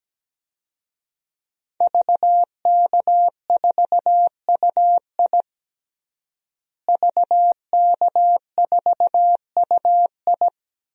Morse code, heard as a keyed tone, about 17 wpm, 700 Hz. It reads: VK4UI VK4UI
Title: DXCC-100 Call Signs - 5 Characters - Repeated 2x - 17wpm